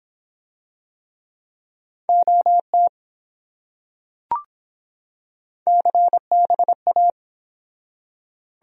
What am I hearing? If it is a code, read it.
Morse code, heard as a keyed tone, about 24 wpm, 700 Hz. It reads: OT CBA